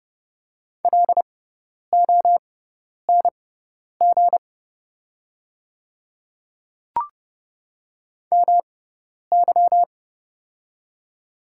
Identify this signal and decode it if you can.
Morse code, heard as a keyed tone, about 30 wpm, spaced as for 10 wpm, 700 Hz. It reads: LONG MY
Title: 6x Character Spacing - Top 100 Words - 30wpm